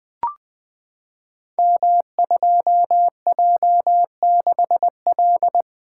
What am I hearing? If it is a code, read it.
Morse code, heard as a keyed tone, about 20 wpm, 700 Hz. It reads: M2J6L
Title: Code Groups - 5 Letter:Numbers - 20wpm